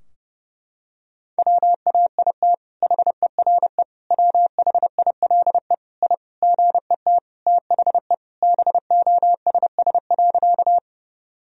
Morse code, heard as a keyed tone, about 32 wpm, 700 Hz. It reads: WAIT HERE WHILE I GET THE BOSS.
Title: Sentences from Top 1000 Words - Review Part 4 - 30wpm